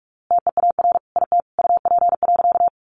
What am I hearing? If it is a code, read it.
Morse code, heard as a keyed tone, about 44 wpm, 700 Hz. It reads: TEAR IT UP.
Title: A Sound Of Thunder 45wpm